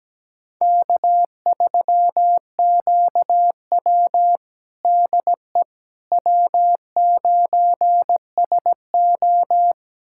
Morse code, heard as a keyed tone, about 17 wpm, 700 Hz. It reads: K3QW DE W9SO
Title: USA Callsign DE Callsign - 4 Characters - 17wpm